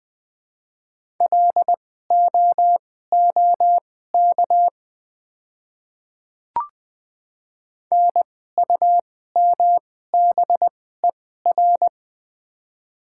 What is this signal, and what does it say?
Morse code, heard as a keyed tone, about 20 wpm, spaced as for 15 wpm, 700 Hz. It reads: LOOK NUMBER
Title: 2x Character Spacing - Top 100 Words - 20wpm